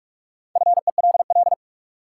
Morse code, heard as a keyed tone, about 46 wpm, 700 Hz. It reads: FELL
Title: Sentences from Top 1600 Words - New Word Drill - 45wpm